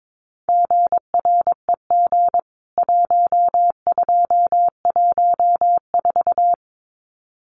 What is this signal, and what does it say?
Morse code, heard as a keyed tone, about 22 wpm, 700 Hz. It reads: GREG 1214
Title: CWT Exchange 22wpm